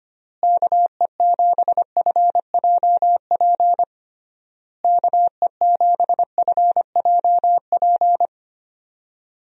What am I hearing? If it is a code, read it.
Morse code, heard as a keyed tone, about 25 wpm, 700 Hz. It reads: KE7FJP KE7FJP
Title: USA Call Signs - 6 Characters - Repeated 3x - 25wpm